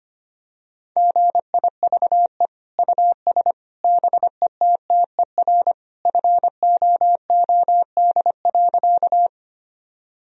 Morse code, heard as a keyed tone, about 25 wpm, 700 Hz. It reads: GIVE US BETTER FOOD.
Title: Sentences from Top 1000 Words - Review Part 1 - 25wpm